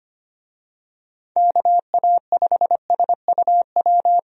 Morse code, heard as a keyed tone, about 25 wpm, 700 Hz. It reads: KA5SUW
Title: USA Call Signs - 6 Characters - 25wpm